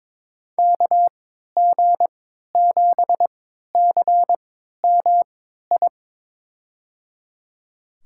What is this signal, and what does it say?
Morse code, heard as a keyed tone, about 22 wpm, spaced as for 12 wpm, 700 Hz. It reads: KG7CMI
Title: USA Call Signs - 6 Characters - 3x Character Spacing - 22wpm